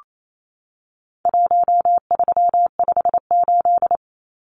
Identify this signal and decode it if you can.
Morse code, heard as a keyed tone, about 28 wpm, 700 Hz. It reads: 1358
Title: Numbers - Contest Style - 4-Digit 28wpm